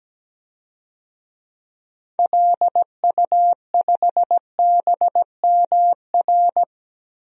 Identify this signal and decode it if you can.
Morse code, heard as a keyed tone, about 17 wpm, 700 Hz. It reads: LU5BMR
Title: DXCC-100 Call Signs - 6 Characters - 17wpm